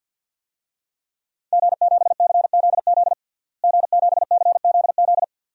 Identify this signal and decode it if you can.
Morse code, heard as a keyed tone, about 47 wpm, 700 Hz. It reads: G7CZZ G7CZZ
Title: DXCC-100 Call Signs - 5 Characters - Repeated 2x - 50wpm